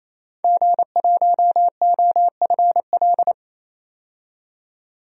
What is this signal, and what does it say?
Morse code, heard as a keyed tone, about 28 wpm, 700 Hz. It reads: G1OFL
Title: DXCC-100 Call Signs - 5 Characters - 28wpm